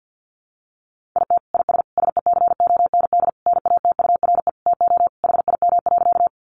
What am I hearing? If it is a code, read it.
Morse code, heard as a keyed tone, about 50 wpm, 700 Hz. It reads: IT IS SECOND NATURE TO HIM.
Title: Sentences from Top 600 Words 50wpm